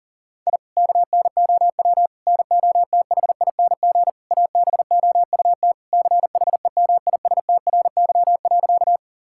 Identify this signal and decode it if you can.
Morse code, heard as a keyed tone, about 37 wpm, 700 Hz. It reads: I KNOW NOTHING ABOUT CHEMISTRY.
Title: Sentences from Top 1000 Words - Review Part 6 - 40wpm